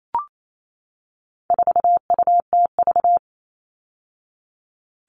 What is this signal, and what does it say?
Morse code, heard as a keyed tone, about 28 wpm, 700 Hz. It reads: 4UTV
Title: All Cut Numbers - 4-Digit 28wpm